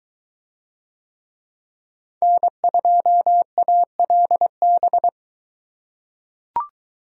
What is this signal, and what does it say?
Morse code, heard as a keyed tone, about 22 wpm, 700 Hz. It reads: N2ALB